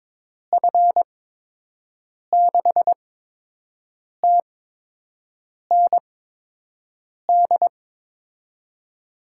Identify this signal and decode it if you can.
Morse code, heard as a keyed tone, about 22 wpm, 700 Hz. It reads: F 6 T N D
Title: USA Call Signs - 6 Characters - 8x Character Spacing - 22wpm